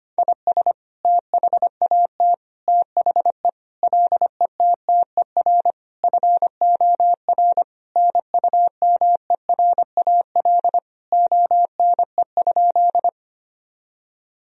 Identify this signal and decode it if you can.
Morse code, heard as a keyed tone, about 25 wpm, 700 Hz. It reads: IS THAT THE LETTER FOR NUMERAL ONE?
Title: Sentences from Top 1000 Words - Review Part 2 - 25wpm